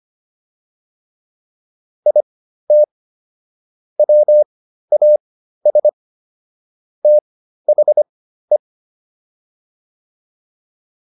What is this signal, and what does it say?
Morse code, heard as a keyed tone, about 25 wpm, spaced as for 13 wpm, 600 Hz. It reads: IT WAS THE